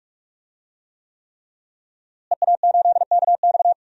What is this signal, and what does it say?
Morse code, heard as a keyed tone, about 46 wpm, 700 Hz. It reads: EA8KX
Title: DXCC-100 Call Signs - 5 Characters - Repeated 3x - 45wpm